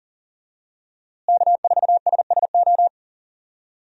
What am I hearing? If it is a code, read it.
Morse code, heard as a keyed tone, about 37 wpm, 700 Hz. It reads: K4SSO
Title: USA Call Signs - 5 Characters - Repeated 2x - 40wpm